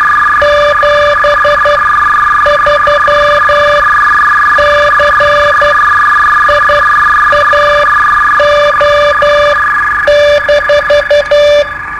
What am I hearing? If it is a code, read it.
Morse code, heard as a keyed tone, about 12 wpm, spaced as for 8 wpm, 590 Hz, with noise. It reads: 73 CIAO-
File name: qso-01